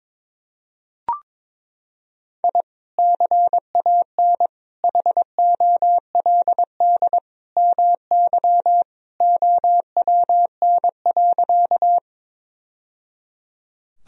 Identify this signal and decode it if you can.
Morse code, heard as a keyed tone, about 22 wpm, 700 Hz. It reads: I CAN HOLD MY OWN.
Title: Sentences from Top 1000 Words - Review Part 2 - 22wpm